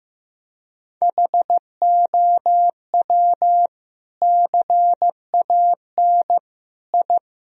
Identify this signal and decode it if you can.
Morse code, heard as a keyed tone, about 15 wpm, 700 Hz. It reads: HOW CAN I
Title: Sentences from Top 1000 Words - Review Part 2 - 15wpm